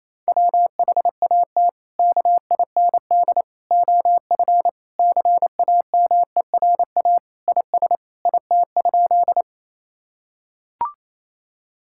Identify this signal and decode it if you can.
Morse code, heard as a keyed tone, about 28 wpm, 700 Hz. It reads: WHAT KIND OF CAMERA IS IT?
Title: Sentences from Top 1000 Words - Review Part 5 - 28wpm